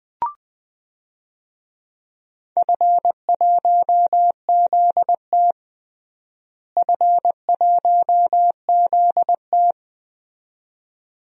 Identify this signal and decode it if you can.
Morse code, heard as a keyed tone, about 20 wpm, 700 Hz. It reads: F1ZT F1ZT
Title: DXCC-100 Call Signs - 4 Characters - Repeated 2x - 20wpm